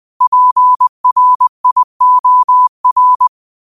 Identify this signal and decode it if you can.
Morse code, heard as a keyed tone, about 20 wpm, 1000 Hz. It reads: PRIOR
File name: qotdfiles / 20wpm